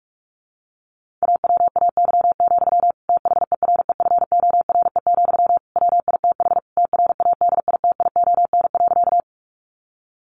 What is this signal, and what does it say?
Morse code, heard as a keyed tone, about 44 wpm, 700 Hz. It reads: AWAY, THEREFORE, WITH TRADITION.
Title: Sentences from Top 1500 Words 45wpm